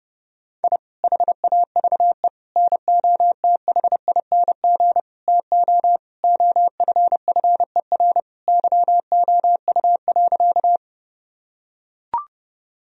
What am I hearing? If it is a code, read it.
Morse code, heard as a keyed tone, about 30 wpm, 700 Hz. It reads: I HAVE NOTHING TO OFFER YOU.
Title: Sentences from Top 1000 Words - Review Part 4 - 30wpm